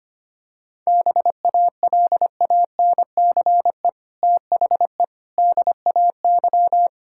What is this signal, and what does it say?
Morse code, heard as a keyed tone, about 25 wpm, 700 Hz. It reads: BALANCE THE DAY
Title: Sentences from Top 1000 Words - Review Part 6 - 25wpm